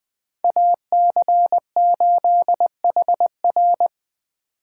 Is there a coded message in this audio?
Yes: Morse code from 0.4 to 3.9 s, heard as a keyed tone, about 20 wpm, 700 Hz. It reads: AC8HR